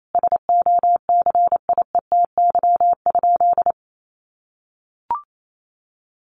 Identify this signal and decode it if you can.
Morse code, heard as a keyed tone, about 28 wpm, 700 Hz. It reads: SOCIETY?
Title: Sentences from Top 700 Words 28wpm